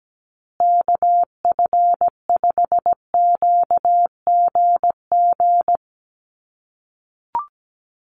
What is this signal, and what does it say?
Morse code, heard as a keyed tone, about 17 wpm, 700 Hz. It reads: KF5QGG
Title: USA Call Signs Repeated 2x 17wpm